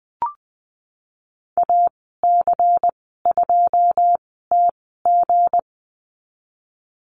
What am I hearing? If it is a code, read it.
Morse code, heard as a keyed tone, about 20 wpm, spaced as for 15 wpm, 700 Hz. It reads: AC2TG
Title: 173 Lesson 28g - Number 9 - Call Signs 20wpm